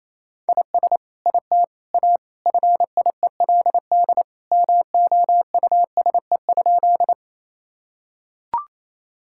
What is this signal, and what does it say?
Morse code, heard as a keyed tone, about 28 wpm, 700 Hz. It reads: IS IT A FIELD MOUSE?
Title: Sentences from Top 1000 Words - Review Part 5 - 28wpm